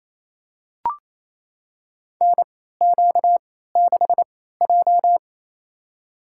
Morse code, heard as a keyed tone, about 28 wpm, spaced as for 16 wpm, 700 Hz. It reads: NQ6J
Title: USA Call Signs - 4 Characters - 3x Character Spacing - 28wpm